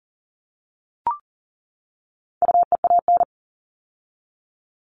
Morse code, heard as a keyed tone, about 40 wpm, 700 Hz. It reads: UEAN
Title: All Cut Numbers - 4-Digit 40wpm